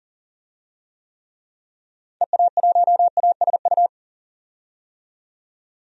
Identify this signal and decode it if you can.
Morse code, heard as a keyed tone, about 40 wpm, 700 Hz. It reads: EA1ASU
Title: DXCC-100 Call Signs - 6 Characters - 40wpm